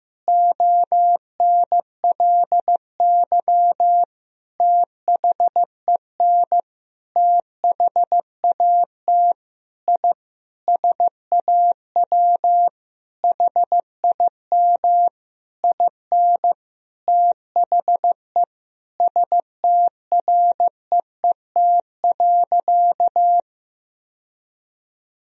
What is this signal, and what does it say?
Morse code, heard as a keyed tone, about 15 wpm, 700 Hz. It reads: ONLY THEN THAT I SAW HIM IN THE STREET.
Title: Sentences from Top 1000 Words - Review Part 3 - 15wpm